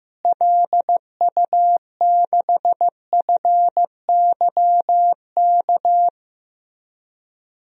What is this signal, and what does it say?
Morse code, heard as a keyed tone, about 15 wpm, 700 Hz. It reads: LU6FYK
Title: DXCC-100 Call Signs - 6 Characters - Repeated 3x - 15wpm